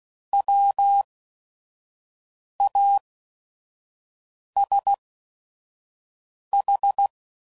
Morse code, heard as a keyed tone, about 16 wpm, spaced as for 5 wpm, 800 Hz. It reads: WASH
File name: Prefixes / By First Letter / Ws